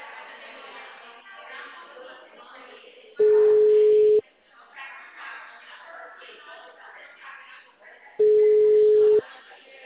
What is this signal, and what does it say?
Ringback tone (Europe, 425 Hz), with speech.
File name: out- -2044- - - .5346